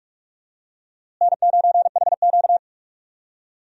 Morse code, heard as a keyed tone, about 46 wpm, 700 Hz. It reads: N9HQ
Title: USA Call Signs - 4 Characters - Repeated 2x - 45wpm